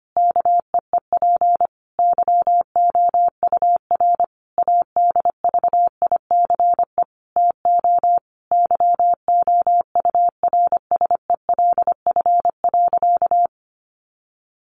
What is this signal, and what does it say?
Morse code, heard as a keyed tone, about 25 wpm, 700 Hz. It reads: KEEP YOUR ADVICE TO YOURSELF.
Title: Sentences from Top 1100 Words 25wpm